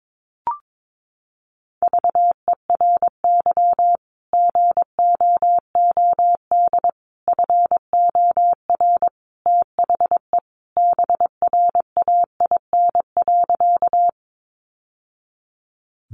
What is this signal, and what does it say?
Morse code, heard as a keyed tone, about 22 wpm, 700 Hz. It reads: VERY GOOD FOR THE BRAIN.
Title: Sentences from Top 1100 Words 22wpm